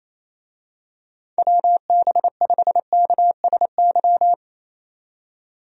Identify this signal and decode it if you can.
Morse code, heard as a keyed tone, about 28 wpm, 700 Hz. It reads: WB5KSY
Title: USA Call Signs - 6 Characters - 28wpm